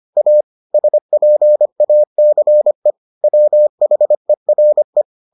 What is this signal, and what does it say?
Morse code, heard as a keyed tone, about 25 wpm, 600 Hz. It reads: A SPACE WHERE